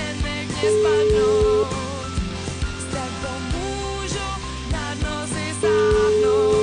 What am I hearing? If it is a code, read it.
Ringback tone (Europe, 425 Hz), with noise.